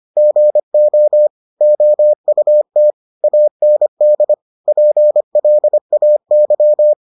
Morse code, heard as a keyed tone, about 25 wpm, 600 Hz. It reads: GO OUT AND PLAY